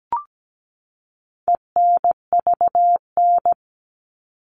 Morse code, heard as a keyed tone, about 17 wpm, 700 Hz. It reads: ENVN
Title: All Cut Numbers - 4-Digit 17wpm